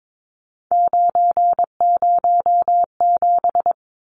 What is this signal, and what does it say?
Morse code, heard as a keyed tone, about 22 wpm, 700 Hz. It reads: 907
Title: Numbers - 3-Digit 22wpm